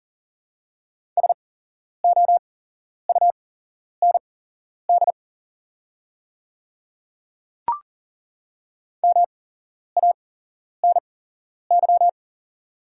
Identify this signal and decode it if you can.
Morse code, heard as a keyed tone, about 40 wpm, spaced as for 11 wpm, 700 Hz. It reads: SOUND MANY
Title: 8x Character Spacing - Top 200 Words - 40wpm